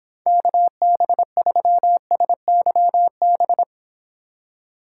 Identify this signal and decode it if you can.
Morse code, heard as a keyed tone, about 31 wpm, 700 Hz. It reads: KB3SYB